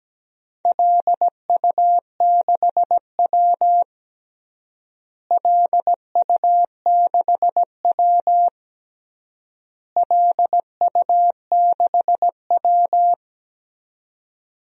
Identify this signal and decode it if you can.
Morse code, heard as a keyed tone, about 17 wpm, 700 Hz. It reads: LU6W LU6W LU6W
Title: DXCC-100 Call Signs - 4 Characters - Repeated 3x - 17wpm